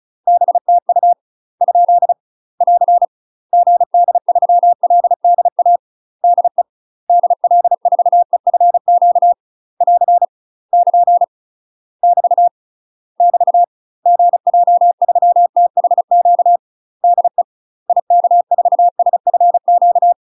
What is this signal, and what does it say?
Morse code, heard as a keyed tone, about 36 wpm, 700 Hz. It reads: BTU ? <AR> GD3LDA DE DL4EFQ <AR> <KN> <BT> <BT> GJ3THQ DE IK4SFQ
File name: UK MASTER QSOs TRAINING / 35WPM / UK MASTER QSOs 0016